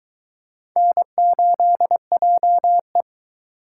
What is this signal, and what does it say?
Morse code, heard as a keyed tone, about 23 wpm, 700 Hz. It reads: N8JE